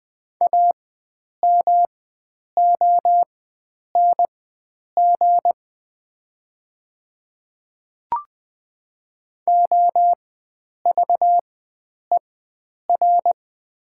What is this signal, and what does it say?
Morse code, heard as a keyed tone, about 20 wpm, spaced as for 9 wpm, 700 Hz. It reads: AMONG OVER
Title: 4x Character Spacing - Top 500 Words - 20wpm